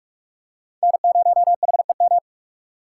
Morse code, heard as a keyed tone, about 43 wpm, 700 Hz. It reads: N0HEM